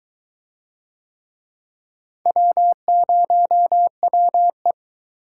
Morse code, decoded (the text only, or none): W0WE